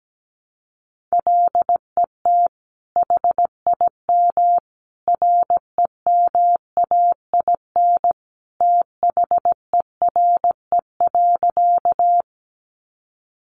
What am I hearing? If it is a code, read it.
Morse code, heard as a keyed tone, about 17 wpm, 700 Hz. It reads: LET HIM REMAIN THERE.
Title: Sentences from Top 1200 Words 17wpm